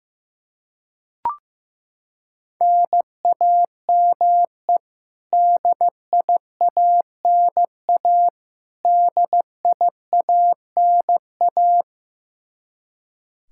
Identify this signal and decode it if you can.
Morse code, heard as a keyed tone, about 15 wpm, 700 Hz. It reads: NAME DIANA DIANA
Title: Practice QSO - Exchange 1 - Beginning - 15wpm